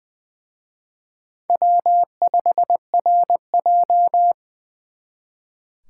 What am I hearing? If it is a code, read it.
Morse code, heard as a keyed tone, about 24 wpm, 700 Hz. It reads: W5RJ
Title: Speed Racing - USA 1x2 Call Signs 20wpm